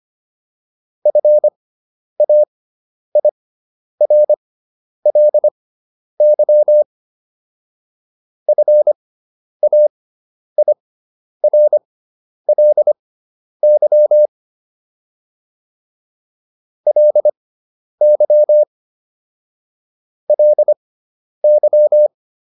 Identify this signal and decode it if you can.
Morse code, heard as a keyed tone, about 25 wpm, spaced as for 10 wpm, 600 Hz. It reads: FAIRLY FAIRLY LY LY